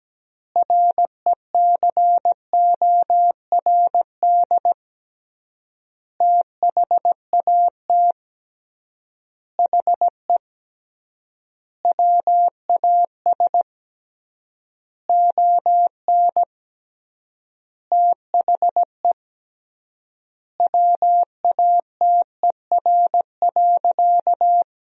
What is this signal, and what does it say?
Morse code, heard as a keyed tone, about 17 wpm, 700 Hz. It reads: RECORD THAT HE WAS ON THE WATER.
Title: 3x Word-Spacing - Sentences from Top 500 Words - 17wpm